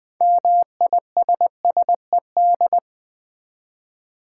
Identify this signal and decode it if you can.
Morse code, heard as a keyed tone, about 20 wpm, 700 Hz. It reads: MISSED
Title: Common QSO Phrases - 4x Word-Spacing - 20wpm